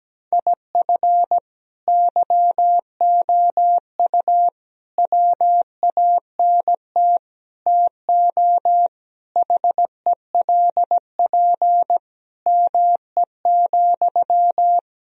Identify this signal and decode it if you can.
Morse code, heard as a keyed tone, about 17 wpm, 700 Hz. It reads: IF YOU WANT TO HELP ME,
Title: Sentences from Top 1000 Words - Review Part 1 - 17wpm